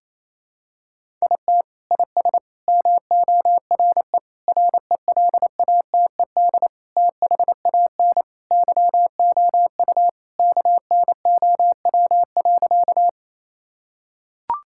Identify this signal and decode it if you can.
Morse code, heard as a keyed tone, about 28 wpm, 700 Hz. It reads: IT IS MORE RELATED THAN YOU KNOW.
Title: Sentences from Top 1000 Words - Review Part 5 - 28wpm